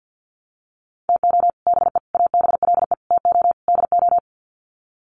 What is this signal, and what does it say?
Morse code, heard as a keyed tone, about 50 wpm, 700 Hz. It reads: TO BE ABLE TO DO
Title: Sets of 5 Words 50wpm